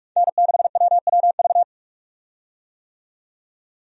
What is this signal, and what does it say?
Morse code, heard as a keyed tone, about 46 wpm, 700 Hz. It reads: N6WWV
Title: USA Call Signs - 5 Characters - 45wpm